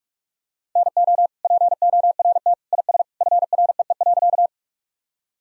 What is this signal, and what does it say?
Morse code, heard as a keyed tone, about 43 wpm, 700 Hz. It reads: NO PORT IS FREE.
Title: Sentences from Top 1000 Words - Review Part 1 - 45wpm